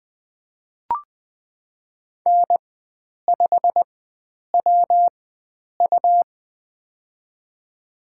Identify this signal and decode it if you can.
Morse code, heard as a keyed tone, about 20 wpm, spaced as for 9 wpm, 700 Hz. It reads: N5WU
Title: USA Call Signs - 4 Characters - 4x Character Spacing - 20wpm